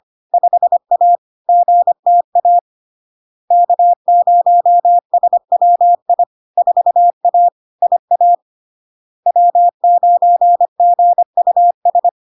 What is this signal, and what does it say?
Morse code, heard as a keyed tone, about 25 wpm, 700 Hz. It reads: 5A GTA K0SWI 4A IA W9GUS
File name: LARK-FD-2022-Log-in-CW-25wpm-700-Hz